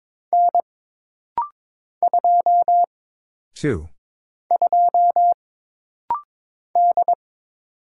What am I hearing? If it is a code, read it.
Morse code, heard as a keyed tone, about 22 wpm, 700 Hz. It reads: N 2 2 D
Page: This rapid-fire timing is geared to push your ICR (Instant Character Recognition) skills.